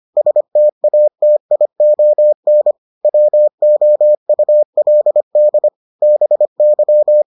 Morse code, heard as a keyed tone, about 25 wpm, 600 Hz. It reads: STATION WOULD BY